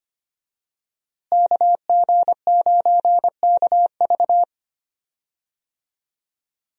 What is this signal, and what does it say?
Morse code, heard as a keyed tone, about 25 wpm, 700 Hz. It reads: KG9KV
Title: USA Call Signs - 5 Characters - Repeated 2x - 25wpm